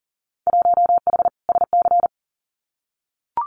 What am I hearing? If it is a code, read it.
Morse code, heard as a keyed tone, about 38 wpm, 700 Hz. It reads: 1H SC
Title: Winter Field Day Exchanges 40wpm